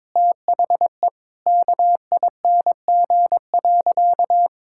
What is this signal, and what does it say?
Morse code, heard as a keyed tone, about 22 wpm, 700 Hz. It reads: THE KING.
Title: Sentences from Top 1000 Words - Review Part 6 - 22wpm